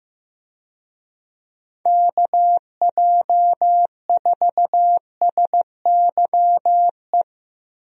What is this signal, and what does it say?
Morse code, heard as a keyed tone, about 15 wpm, 700 Hz. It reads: KJ4SYE
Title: USA Call Signs - 6 Characters - Repeated 2x - 15wpm